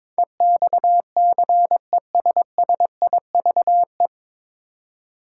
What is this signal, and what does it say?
Morse code, heard as a keyed tone, about 22 wpm, 700 Hz. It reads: EXCESSIVE
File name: 9 Letter Words - First 3 Letters Given - 22wpm